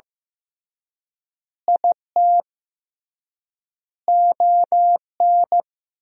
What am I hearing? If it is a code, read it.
Morse code, heard as a keyed tone, about 16 wpm, 700 Hz. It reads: IT ON
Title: 3x Word-Spacing - Sentences from Top 400 Words - 15wpm